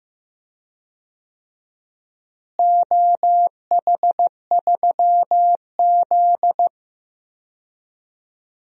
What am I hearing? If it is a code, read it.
Morse code, heard as a keyed tone, about 15 wpm, 700 Hz. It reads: OH3Z
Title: DXCC-100 Call Signs - 4 Characters - Repeated 2x - 15wpm